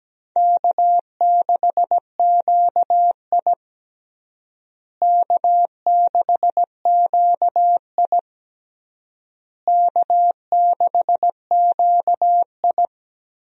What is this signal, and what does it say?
Morse code, heard as a keyed tone, about 17 wpm, 700 Hz. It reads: K6QI K6QI K6QI
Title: USA Call Signs - 4 Characters - Repeated 3x - 17wpm